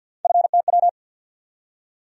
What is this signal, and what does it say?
Morse code, heard as a keyed tone, about 53 wpm, 700 Hz. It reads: FTW
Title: USA Call Sign Postfixes - 3 Letters 50wpm